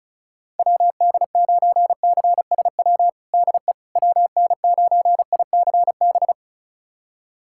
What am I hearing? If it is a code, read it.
Morse code, heard as a keyed tone, about 36 wpm, 700 Hz. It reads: WD9CSW DE WN9ICB
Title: USA Callsign DE Callsign 35wpm